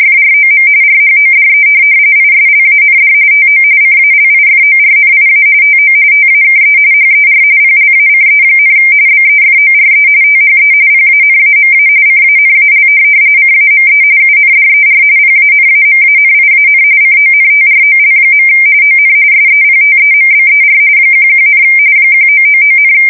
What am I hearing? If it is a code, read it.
RTTY, about 45.45 baud. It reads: FROM RHODE ISLAND TO CALIFORNIA TO TAKE HER PLACE, BEFORE THE PANAMA CANAL CAME UNDER PANAMANIAN CONTROL. ON SEPTEMBER 11, 1999,
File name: fsk170 / USS-New-Jersey